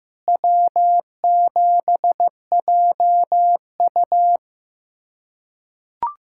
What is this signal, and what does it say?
Morse code, heard as a keyed tone, about 15 wpm, 700 Hz. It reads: W7JU
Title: USA Callsign DE Callsign 15wpm